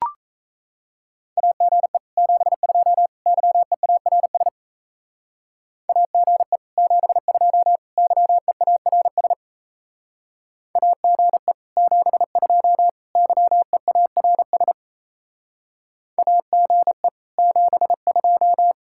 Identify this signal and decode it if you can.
Morse code, heard as a keyed tone, about 35 wpm, 700 Hz. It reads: AGE 72 YEARS AGE 72 YEARS AGE 72 YEARS AGE 72
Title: Speed-Racing Common QSO Phrases Encore - 28wpm